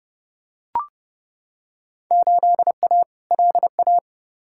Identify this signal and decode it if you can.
Morse code, heard as a keyed tone, about 31 wpm, 700 Hz. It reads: 8A LA
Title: Field Day Exchanges Encore - 30wpm